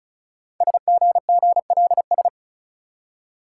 Morse code, heard as a keyed tone, about 35 wpm, 700 Hz. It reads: SGGLS
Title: Code Groups - 5 Letters - 35wpm